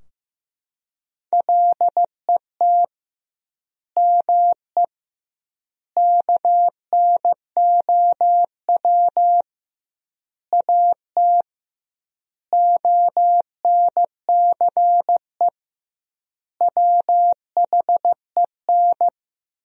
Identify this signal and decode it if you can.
Morse code, heard as a keyed tone, about 15 wpm, 700 Hz. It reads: LET ME KNOW AT ONCE WHEN
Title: 2x Word-Spacing - Sentences from Top 300 Words - 15wpm